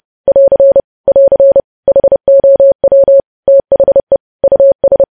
Morse code, heard as a keyed tone, about 30 wpm, 550 Hz. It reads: <AR> <AR> HOW THE US
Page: news headlines in morse code at 30 wpm